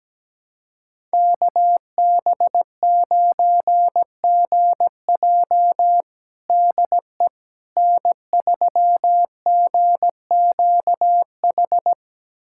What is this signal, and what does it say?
Morse code, heard as a keyed tone, about 17 wpm, 700 Hz. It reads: KB9GJ DE N3GQH
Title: USA Callsign DE Callsign - 5 Characters - 17wpm